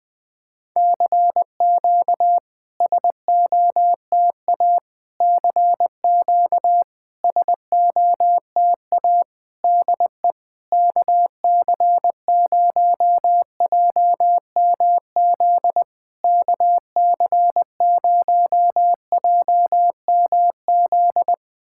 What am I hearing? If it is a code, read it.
Morse code, heard as a keyed tone, about 21 wpm, 700 Hz. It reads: CQ SOTA CQ SOTA DE KC0JMZ KC0JMZ
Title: SOTA QSO Practice 20wpm